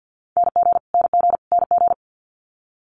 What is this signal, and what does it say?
Morse code, heard as a keyed tone, about 50 wpm, 700 Hz. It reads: NG NG NG
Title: Multi-letter Phonemes Repeated 3x 50wpm